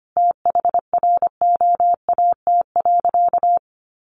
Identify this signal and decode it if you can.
Morse code, heard as a keyed tone, about 25 wpm, 700 Hz. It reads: THROAT.
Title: A Sound Of Thunder 25wpm